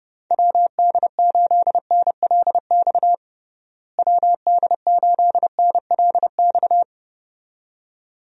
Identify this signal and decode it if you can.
Morse code, heard as a keyed tone, about 30 wpm, 700 Hz. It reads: WD8NLX WD8NLX
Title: USA Call Signs - 6 Characters - Repeated 2x - 30wpm